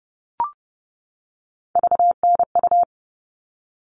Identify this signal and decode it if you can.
Morse code, heard as a keyed tone, about 30 wpm, 700 Hz. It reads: VNU
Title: All Cut Numbers - 3-Digit 30wpm